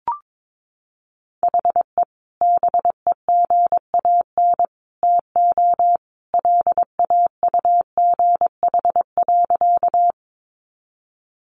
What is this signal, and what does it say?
Morse code, heard as a keyed tone, about 22 wpm, 700 Hz. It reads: HE BEGAN TO LAUGH.
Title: Sentences from Top 500 Words 22wpm